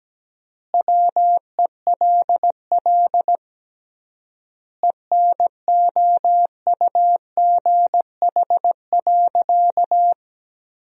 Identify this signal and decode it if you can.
Morse code, heard as a keyed tone, about 17 wpm, 700 Hz. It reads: WELL ENOUGH.
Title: 3x Word-Spacing - Sentences from Top 400 Words - 17wpm